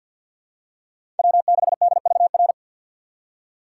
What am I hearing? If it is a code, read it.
Morse code, heard as a keyed tone, about 49 wpm, 690 Hz. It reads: W6DVR
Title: USA Call Signs - 4x Character Spacing - 50wpm